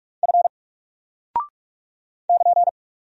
Morse code, heard as a keyed tone, about 48 wpm, 700 Hz. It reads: F <KN>
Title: Single Character (Letter:Number:Punctuation:Prosign) Alt 45wpm